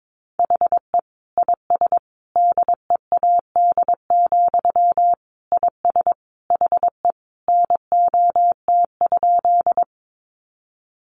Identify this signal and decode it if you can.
Morse code, heard as a keyed tone, about 22 wpm, 700 Hz. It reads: HE IS DEAD, IS HE NOT?
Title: Sentences from Top 1500 Words 22wpm